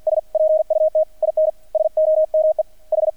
Morse code, heard as a keyed tone, about 50 wpm, 630 Hz. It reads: S PUT IN SOME H
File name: archived / CRSnet